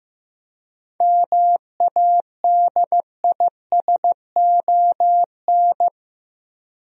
Morse code, heard as a keyed tone, about 15 wpm, 700 Hz. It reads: MADISON
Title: USA Common Names - 15wpm